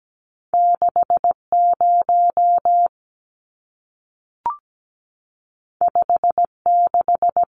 Morse code, heard as a keyed tone, about 17 wpm, 700 Hz. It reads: 60 56
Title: Numbers - 2-Digit 17wpm